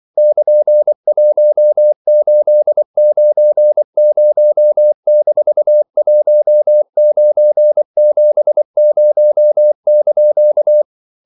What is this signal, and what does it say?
Morse code, heard as a keyed tone, about 24 wpm, 600 Hz. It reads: <KN>1890-1970)
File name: CW / Practice / EnglishQuotations0324